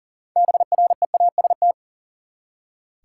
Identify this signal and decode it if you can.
Morse code, heard as a keyed tone, about 40 wpm, 700 Hz. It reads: BREAST
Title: Six Letter Words - 40wpm